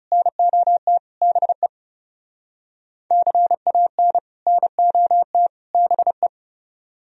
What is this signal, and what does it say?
Morse code, heard as a keyed tone, about 40 wpm, 700 Hz. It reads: NOT BE CAN NOT BE